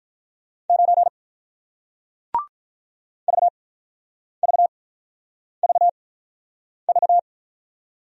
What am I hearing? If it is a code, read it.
Morse code, heard as a keyed tone, about 48 wpm, spaced as for 9 wpm, 700 Hz. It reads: 9 VVVV